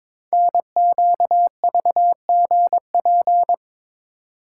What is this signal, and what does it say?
Morse code, heard as a keyed tone, about 22 wpm, 700 Hz. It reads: NQVGP
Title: Code Groups - 5 Letters - 22wpm